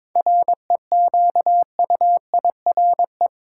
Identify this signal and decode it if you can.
Morse code, heard as a keyed tone, about 22 wpm, 700 Hz. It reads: REQUIRE
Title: Sentences from Top 900 Words - New Word Drill - 22wpm